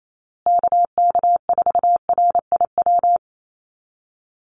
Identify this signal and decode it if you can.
Morse code, heard as a keyed tone, about 28 wpm, 700 Hz. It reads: KK4RIW
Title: USA Call Signs Repeated 3x 28wpm